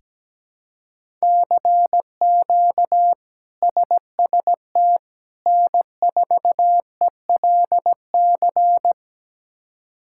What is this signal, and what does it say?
Morse code, heard as a keyed tone, about 17 wpm, 700 Hz. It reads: CQ SST N4ELC
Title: K1USN SST QSO Practice - 17wpm